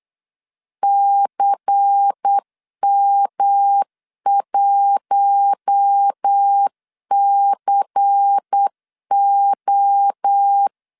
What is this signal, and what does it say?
Morse code, heard as a keyed tone, about 8 wpm, 800 Hz. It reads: CM1CO
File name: CW / DET / 1 9 1500C S A 8wpm 5637